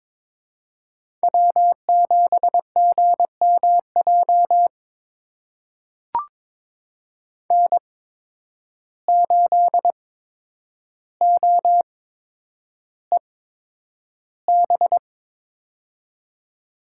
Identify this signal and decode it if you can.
Morse code, heard as a keyed tone, about 22 wpm, spaced as for 9 wpm, 700 Hz. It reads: W7GMJ N8OEB
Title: USA Call Signs - 5 Characters - 8x Character Spacing - 22wpm